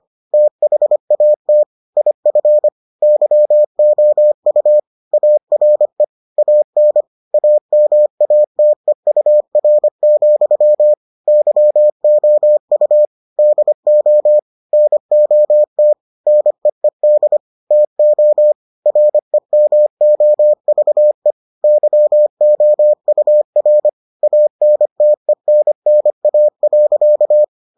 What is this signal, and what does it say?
Morse code, heard as a keyed tone, about 25 wpm, 600 Hz. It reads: THAT IF YOU ARE AN AMATEUR, YOU DO NOT NEED TO REMOVE YOUR ANTENNA.